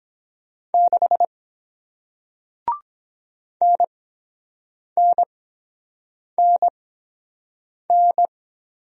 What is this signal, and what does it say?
Morse code, heard as a keyed tone, about 22 wpm, spaced as for 7 wpm, 700 Hz. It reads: 6 NNNN